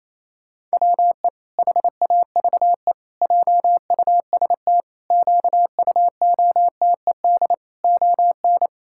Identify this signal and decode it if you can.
Morse code, heard as a keyed tone, about 27 wpm, 700 Hz. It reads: WE HAVE JUST QUOTED ON
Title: Pangram Challenge v2 - 28wpm